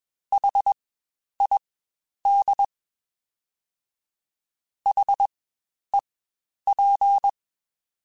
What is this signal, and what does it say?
Morse code, heard as a keyed tone, about 21 wpm, spaced as for 10 wpm, 800 Hz. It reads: HID HEP
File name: Spaced Alphas / Spaced Alphas mp3s / Hs sp @20